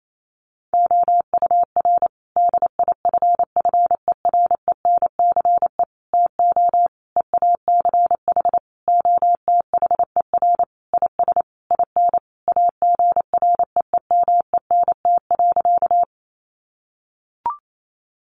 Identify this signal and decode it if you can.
Morse code, heard as a keyed tone, about 28 wpm, 700 Hz. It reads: OUR DIFFERENCE TO EACH OTHER IS IN AGREEMENT.
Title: Sentences from Top 1500 Words 28wpm